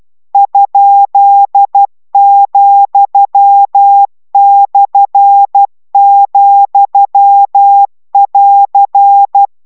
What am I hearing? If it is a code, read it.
Morse code, heard as a keyed tone, about 12 wpm, 800 Hz. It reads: ?,/,<AR>